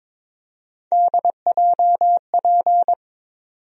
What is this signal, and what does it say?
Morse code, heard as a keyed tone, about 22 wpm, 700 Hz. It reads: DJP
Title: USA Call Sign Postfixes - 3 Letters 22wpm